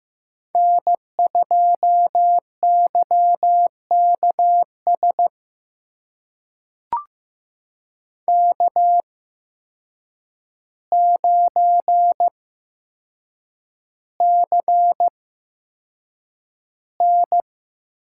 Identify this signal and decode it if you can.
Morse code, heard as a keyed tone, about 15 wpm, spaced as for 7 wpm, 700 Hz. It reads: N2YKS K9CN
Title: USA Call Signs - 5 Characters - 8x Character Spacing - 15wpm